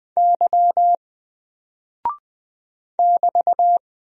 Morse code, heard as a keyed tone, about 20 wpm, 700 Hz. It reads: Y <BT>
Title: Single Character (Letter:Number:Punctuation:Prosign) 20wpm